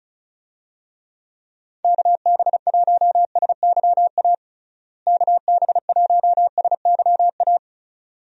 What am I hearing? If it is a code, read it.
Morse code, heard as a keyed tone, about 36 wpm, 700 Hz. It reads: KB1SYA KB1SYA
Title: USA Call Signs - 6 Characters - Repeated 2x - 35wpm